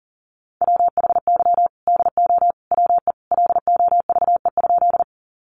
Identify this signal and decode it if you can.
Morse code, heard as a keyed tone, about 40 wpm, 700 Hz. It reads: WHY DO WE LOVE?
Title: Sentences from Top 400 Words 40wpm